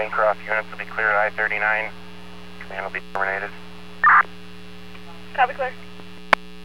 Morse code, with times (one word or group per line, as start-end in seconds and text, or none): none